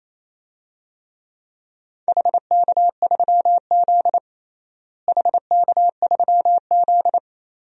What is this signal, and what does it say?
Morse code, heard as a keyed tone, about 28 wpm, 700 Hz. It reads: HK3Z HK3Z
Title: DXCC-100 Call Signs - 4 Characters - Repeated 2x - 28wpm